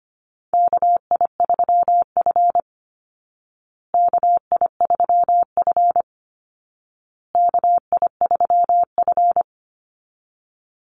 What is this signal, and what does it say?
Morse code, heard as a keyed tone, about 25 wpm, 700 Hz. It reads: KI3F KI3F KI3F
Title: Call Signs Repeated 2x - 25wpm